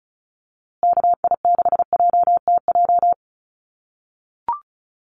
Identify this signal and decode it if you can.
Morse code, heard as a keyed tone, about 34 wpm, 700 Hz. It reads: KI6JTJ
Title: USA Call Signs Repeated 2x 35wpm